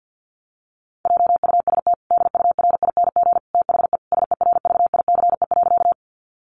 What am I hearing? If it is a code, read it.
Morse code, heard as a keyed tone, about 50 wpm, 700 Hz. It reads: JUST DURING THE SERVICE.
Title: Sentences from Top 700 Words 50wpm